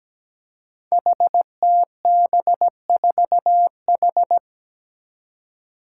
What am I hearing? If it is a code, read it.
Morse code, heard as a keyed tone, about 17 wpm, 700 Hz. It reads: HTB4H
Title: Five Challenging Letter-Number Combinations 17wpm